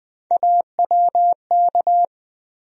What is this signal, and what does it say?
Morse code, heard as a keyed tone, about 21 wpm, 700 Hz. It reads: AWK
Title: USA Call Sign Postfixes - 3 Letters 20wpm